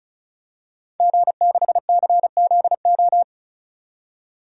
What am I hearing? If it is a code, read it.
Morse code, heard as a keyed tone, about 36 wpm, 700 Hz. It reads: G6CZO
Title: DXCC-100 Call Signs - 5 Characters - 35wpm